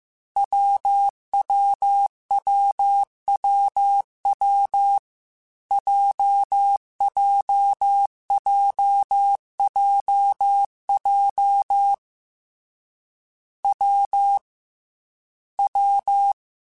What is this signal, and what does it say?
Morse code, heard as a keyed tone, about 15 wpm, 800 Hz. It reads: WWWWW JJJJJ W W
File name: Difficult Combos / Difficult Combos-mp3s / WPJ WPJ1 @13